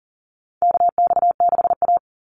Morse code, heard as a keyed tone, about 40 wpm, 700 Hz. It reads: KX6A
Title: NA SOTA Chaser Callsigns 40wpm